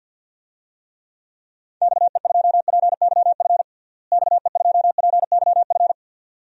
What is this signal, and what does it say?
Morse code, heard as a keyed tone, about 47 wpm, 700 Hz. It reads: XE2PYF XE2PYF
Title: DXCC-100 Call Signs - 6 Characters - Repeated 2x - 50wpm